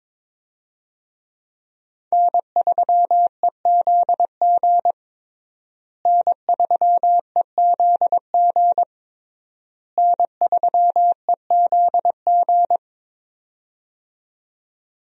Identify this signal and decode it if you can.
Morse code, heard as a keyed tone, about 22 wpm, 700 Hz. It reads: N3EZG N3EZG N3EZG
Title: USA Call Signs - 5 Characters - Repeated 3x - 22wpm